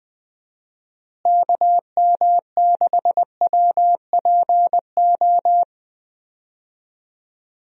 Morse code, heard as a keyed tone, about 20 wpm, 700 Hz. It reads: KM6WPO
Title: USA Call Signs - 6 Characters - Repeated 2x - 20wpm